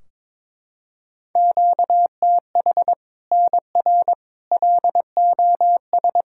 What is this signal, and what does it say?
Morse code, heard as a keyed tone, about 21 wpm, 700 Hz. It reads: QTH NR LOS
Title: Practice QSO - Exchange 1 - Beginning - 22wpm